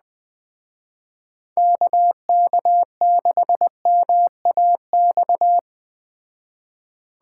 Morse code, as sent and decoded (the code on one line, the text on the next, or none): -.- -.- -.... -- .- -..-
KK6MAX